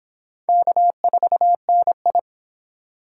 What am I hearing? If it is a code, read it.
Morse code, heard as a keyed tone, about 24 wpm, 700 Hz. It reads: K4NI